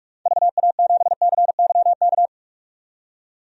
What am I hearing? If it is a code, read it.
Morse code, heard as a keyed tone, about 45 wpm, 700 Hz. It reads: VA7CYK
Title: DXCC-100 Call Signs - 6 Characters - 45wpm